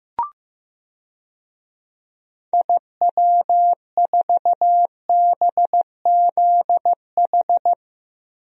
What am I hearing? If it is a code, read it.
Morse code, heard as a keyed tone, about 15 wpm, 700 Hz. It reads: IW4BZH
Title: DXCC-100 Call Signs - 6 Characters - 15wpm